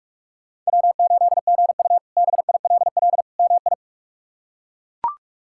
Morse code, heard as a keyed tone, about 42 wpm, 700 Hz. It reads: W8GU BILL MI
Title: K1USN SST QSO Practice - 45wpm